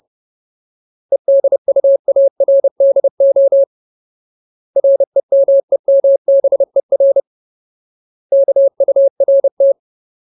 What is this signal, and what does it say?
Morse code, heard as a keyed tone, about 30 wpm, 550 Hz. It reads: EDUARDO REMEMBER KURT
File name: Ultimate Word List / 30WPM Ultimate